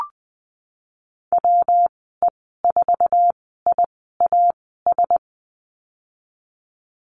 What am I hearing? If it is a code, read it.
Morse code, heard as a keyed tone, about 19 wpm, spaced as for 14 wpm, 700 Hz. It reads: WE4IAS
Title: 147 Lesson 24g - Letter y - Call Signs 20wpm